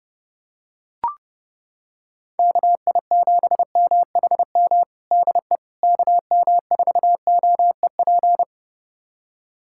Morse code, heard as a keyed tone, about 30 wpm, 700 Hz. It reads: KI7MHM DE KM4OEP
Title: USA Callsign DE Callsign - 6 Characters - 30wpm